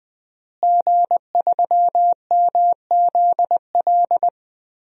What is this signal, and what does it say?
Morse code, heard as a keyed tone, about 20 wpm, 700 Hz. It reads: G3MZL
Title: DXCC-100 Call Signs - 5 Characters - Repeated 3x - 20wpm